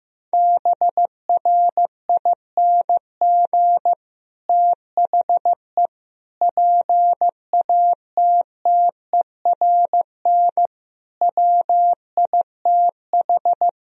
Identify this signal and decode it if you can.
Morse code, heard as a keyed tone, about 16 wpm, 700 Hz. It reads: BRING THE PATTERN WITH
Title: Sentences from Top 1000 Words - Review Part 2 - 15wpm